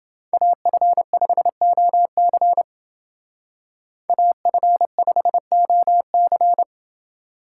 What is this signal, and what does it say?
Morse code, heard as a keyed tone, about 29 wpm, 700 Hz. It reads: AF5OC AF5OC